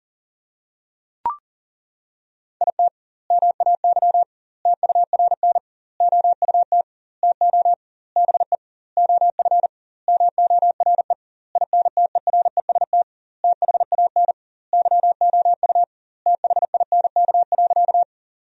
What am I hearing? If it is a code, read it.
Morse code, heard as a keyed tone, about 40 wpm, 700 Hz. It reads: IT MAY TURN OUT TO BE OF MORE INTEREST THAN YOU THINK.
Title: 2x Word-Spacing - Sentences from Top 400 Words - 40wpm